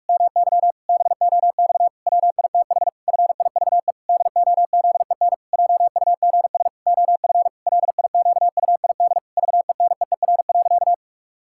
Morse code, heard as a keyed tone, about 46 wpm, 700 Hz. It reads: MY BOX WITH FIVE DOZEN JUGS OF LIQUID VENEER.
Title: Pangram Challenge v2 - 45wpm